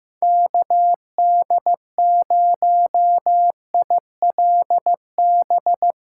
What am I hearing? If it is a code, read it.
Morse code, heard as a keyed tone, about 16 wpm, 700 Hz. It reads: KD0ILB
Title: Speed Racing - USA 2x3 Call Signs 15wpm